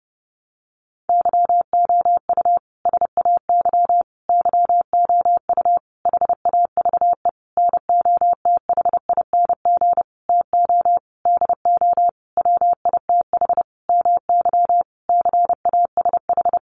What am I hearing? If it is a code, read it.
Morse code, heard as a keyed tone, about 30 wpm, 700 Hz. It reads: YOU SAY YOU HAVE NOTHING TO DO WITH MY CASH
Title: Sentences from Top 700 Words 30wpm